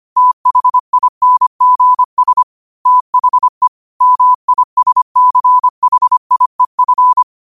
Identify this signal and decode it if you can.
Morse code, heard as a keyed tone, about 25 wpm, 1000 Hz. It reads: THINGS THE MISCHIEF
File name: qotdfiles / 25wpm